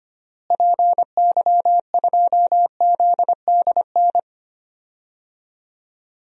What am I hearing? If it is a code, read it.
Morse code, heard as a keyed tone, about 25 wpm, 700 Hz. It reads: PY2ZDN
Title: DXCC-100 Call Signs - 6 Characters - Repeated 3x - 25wpm